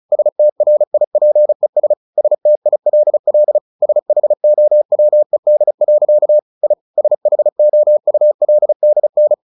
Morse code, heard as a keyed tone, about 34 wpm, 600 Hz. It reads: STRIPES STILL SHOWED. I SHOULDN
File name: the old man and the sea 2 35wpm